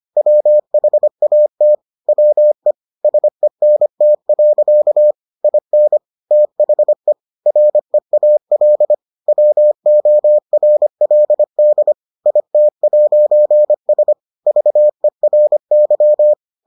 Morse code, heard as a keyed tone, about 25 wpm, 600 Hz. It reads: WHAT WE SENT. IN THE REAL WORLD IT'S VERY